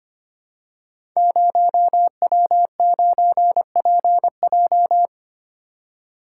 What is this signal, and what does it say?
Morse code, heard as a keyed tone, about 25 wpm, 700 Hz. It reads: 0W9PJ
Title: Code Groups - 5 Letter:Numbers - 25wpm